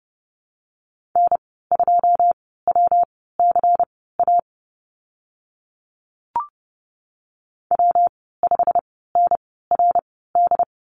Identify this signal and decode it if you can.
Morse code, heard as a keyed tone, about 30 wpm, spaced as for 17 wpm, 700 Hz. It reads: N2WCA W5NRD
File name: Lesson 19g - Letter w - Call Signs 30wpm